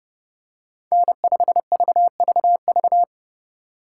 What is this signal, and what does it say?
Morse code, heard as a keyed tone, about 30 wpm, 700 Hz. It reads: N5VVV
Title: USA Call Signs - 5 Characters - 3x Character Spacing - 30wpm